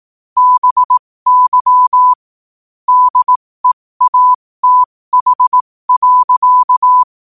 Morse code, heard as a keyed tone, about 18 wpm, spaced as for 15 wpm, 1000 Hz. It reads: BY DEATH.
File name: qotdfiles / 15wpm